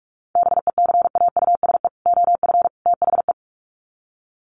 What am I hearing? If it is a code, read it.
Morse code, heard as a keyed tone, about 44 wpm, 700 Hz. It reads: BECAUSE OF THE
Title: Sets of 3 Words 45wpm